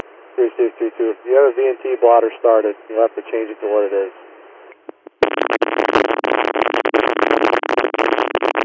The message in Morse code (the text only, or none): none